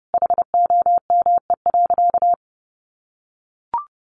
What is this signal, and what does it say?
Morse code, heard as a keyed tone, about 30 wpm, 700 Hz. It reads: HOME.
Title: Sentences from Top 400 Words 30wpm